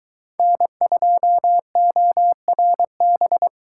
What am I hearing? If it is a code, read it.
Morse code, heard as a keyed tone, about 23 wpm, 700 Hz. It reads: N2ORB